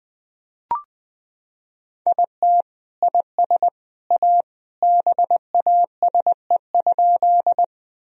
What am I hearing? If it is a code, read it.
Morse code, heard as a keyed tone, about 20 wpm, 700 Hz. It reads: IT IS A BASE?
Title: Sentences from Top 1000 Words - Review Part 1 - 20wpm